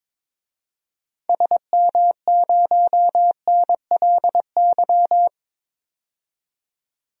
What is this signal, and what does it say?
Morse code, heard as a keyed tone, about 22 wpm, 700 Hz. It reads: SM0NLY
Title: DXCC-100 Call Signs - 6 Characters - 22wpm